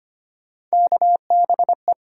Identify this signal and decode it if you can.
Morse code, heard as a keyed tone, about 25 wpm, 700 Hz. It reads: KBE
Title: USA Call Sign Postfixes - 3 Letters 25wpm